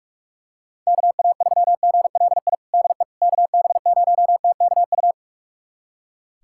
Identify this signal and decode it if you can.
Morse code, heard as a keyed tone, about 45 wpm, 700 Hz. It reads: KA3GLI DE KB0TKU
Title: USA Callsign DE Callsign - 6 Characters - 45wpm